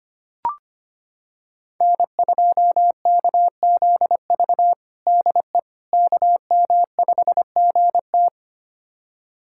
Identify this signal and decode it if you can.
Morse code, heard as a keyed tone, about 25 wpm, 700 Hz. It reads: N2KZV DE KM5GT
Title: USA Callsign DE Callsign - 5 Characters - 25wpm